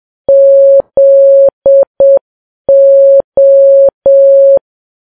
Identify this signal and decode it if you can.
Morse code, heard as a keyed tone, about 7 wpm, 550 Hz. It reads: ZO
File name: Files Web tls / Morse on9fv / morse7kps